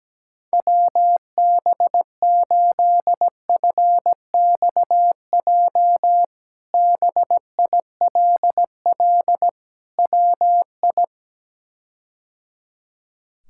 Morse code, heard as a keyed tone, about 17 wpm, 700 Hz. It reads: WB8FXJ BILL WI
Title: K1USN SST QSO Practice - 17wpm